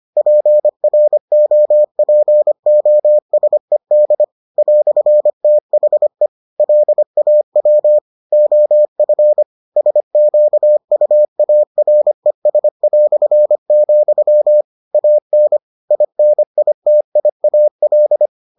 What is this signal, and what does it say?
Morse code, heard as a keyed tone, about 25 wpm, 600 Hz. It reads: PROPOSED "THE LAW OF SQUARES", AN INITIAL